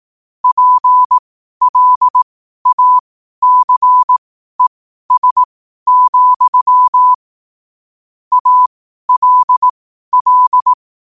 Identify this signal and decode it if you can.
Morse code, heard as a keyed tone, about 18 wpm, spaced as for 12 wpm, 1000 Hz. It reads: PLACES, ALL